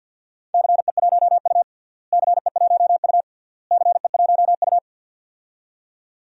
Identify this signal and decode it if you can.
Morse code, heard as a keyed tone, about 47 wpm, 700 Hz. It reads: CE1U CE1U CE1U
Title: DXCC-100 Call Signs - 4 Characters - Repeated 3x - 50wpm